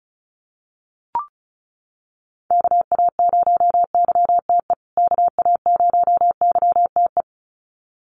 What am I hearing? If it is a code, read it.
Morse code, heard as a keyed tone, about 34 wpm, 700 Hz. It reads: KA0YTE KA0YTE
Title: USA Call Signs Repeated 2x 35wpm